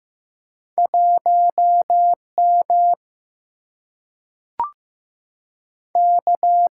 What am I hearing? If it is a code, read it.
Morse code, heard as a keyed tone, about 15 wpm, 700 Hz. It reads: 1 M K
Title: Single-Letter Single-Number 15wpm